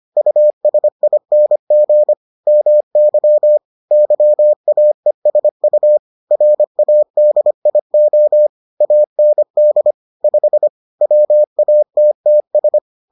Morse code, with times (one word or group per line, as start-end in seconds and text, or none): 0.2-2.1 USING
2.5-3.6 MY
3.9-6.0 YAESU
6.3-8.5 RADIO
8.8-9.9 AND
10.2-10.7 5
11.0-12.8 WATTS